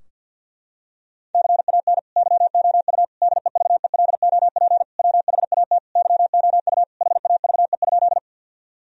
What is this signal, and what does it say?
Morse code, heard as a keyed tone, about 47 wpm, 700 Hz. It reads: CAN YOU DEVELOP WHAT YOU HAVE?
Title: Sentences from Top 1000 Words - Review Part 2 - 50wpm